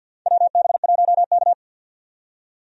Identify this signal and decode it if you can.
Morse code, heard as a keyed tone, about 48 wpm, 700 Hz. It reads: WB1K
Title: USA Call Signs - 4 Characters - 50wpm